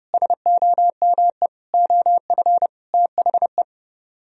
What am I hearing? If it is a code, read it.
Morse code, heard as a keyed tone, about 38 wpm, 700 Hz. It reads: SOME OF THE
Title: Speed-Racing - Sets of 4 Words 30wpm